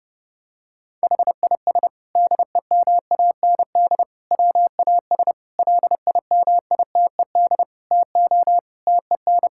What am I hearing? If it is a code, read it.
Morse code, heard as a keyed tone, about 30 wpm, 700 Hz. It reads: HIS DEMAND WAS LIMITED TO TEN
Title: Sentences from Top 1000 Words - Review Part 6 - 30wpm